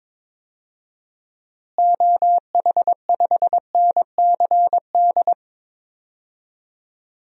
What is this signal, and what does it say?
Morse code, heard as a keyed tone, about 22 wpm, 700 Hz. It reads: OH5NCD
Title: DXCC-100 Call Signs - 6 Characters - Repeated 3x - 22wpm